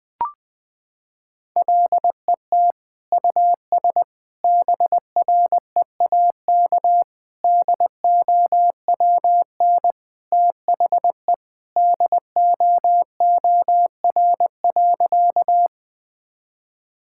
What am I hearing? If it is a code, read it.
Morse code, heard as a keyed tone, about 19 wpm, 700 Hz. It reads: LET US BREAK DOWN THE DOOR.
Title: Sentences from Top 1000 Words - Review Part 6 - 20wpm